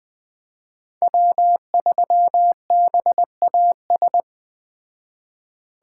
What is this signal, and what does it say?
Morse code, heard as a keyed tone, about 20 wpm, 700 Hz. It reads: W3BAS